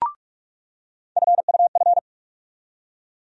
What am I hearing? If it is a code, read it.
Morse code, heard as a keyed tone, about 47 wpm, 700 Hz. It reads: FUF
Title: USA Call Sign Postfixes - 3 Letters 45wpm